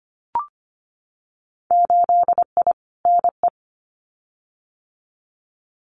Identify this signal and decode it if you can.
Morse code, heard as a keyed tone, about 25 wpm, 700 Hz. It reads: 8I NE
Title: Winter Field Day Exchanges Encore 25wpm